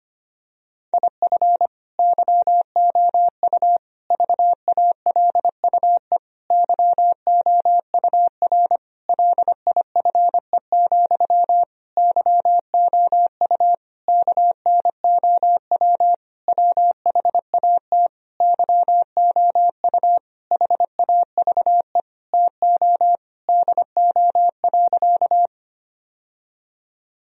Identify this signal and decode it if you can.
Morse code, heard as a keyed tone, about 25 wpm, 700 Hz. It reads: IF YOU VALUE YOUR LIFE, YOU KNOW WHAT YOU HAVE TO DO.
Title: Sentences from Top 1000 Words - Review Part 3 - 25wpm